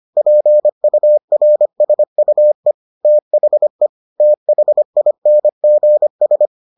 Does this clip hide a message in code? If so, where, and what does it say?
Yes: Morse code from 0.2 to 6.5 s, heard as a keyed tone, about 25 wpm, 600 Hz. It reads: PURSUE THE THINGS